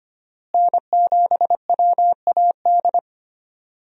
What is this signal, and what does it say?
Morse code, heard as a keyed tone, about 25 wpm, 700 Hz. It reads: N7WAD
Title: USA Call Signs - 5 Characters - Repeated 2x - 25wpm